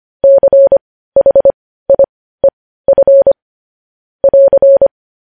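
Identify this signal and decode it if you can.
Morse code, heard as a keyed tone, about 25 wpm, spaced as for 15 wpm, 550 Hz. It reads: CHIEF <AR>
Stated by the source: Morse code news headlines at 15 wpm.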